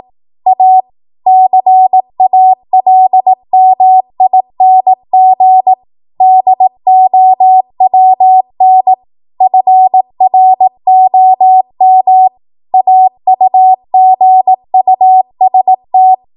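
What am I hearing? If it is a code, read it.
Morse code, heard as a keyed tone, about 18 wpm, 750 Hz. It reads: A CALMING DOWN FROM AUGUST
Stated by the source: ARRL 18 wpm Morse code practice archive transmission.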